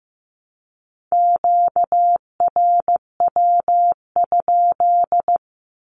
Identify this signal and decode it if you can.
Morse code, heard as a keyed tone, about 15 wpm, 700 Hz. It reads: QRW?
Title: Q-Codes Repeated 3x 15wpm